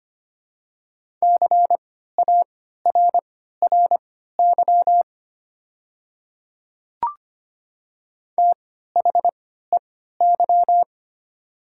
Morse code, heard as a keyed tone, about 25 wpm, spaced as for 14 wpm, 700 Hz. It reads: CARRY THEY
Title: 3x Character Spacing - Top 300 Words - 25wpm